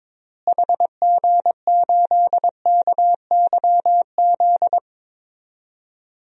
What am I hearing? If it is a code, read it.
Morse code, heard as a keyed tone, about 22 wpm, 700 Hz. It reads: HG8KYZ
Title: DXCC-100 Call Signs - 6 Characters - Repeated 2x - 22wpm